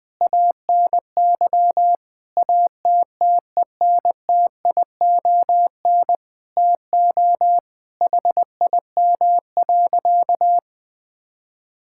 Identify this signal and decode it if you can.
Morse code, heard as a keyed tone, about 20 wpm, 700 Hz. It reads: ANY ATTENTION TO HIM.
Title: Sentences from Top 1000 Words - Review Part 6 - 20wpm